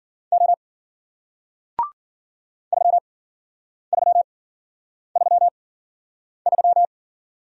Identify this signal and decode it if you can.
Morse code, heard as a keyed tone, about 53 wpm, spaced as for 9 wpm, 700 Hz. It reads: O 3333